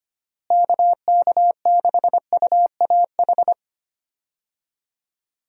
Morse code, heard as a keyed tone, about 25 wpm, 700 Hz. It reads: KK6UAH
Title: USA Call Signs - 6 Characters - Repeated 3x - 25wpm